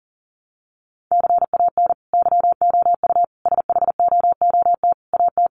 Morse code, heard as a keyed tone, about 40 wpm, 700 Hz. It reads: CAN YOU SHOOT AT
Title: Sentences from Top 1200 Words 40wpm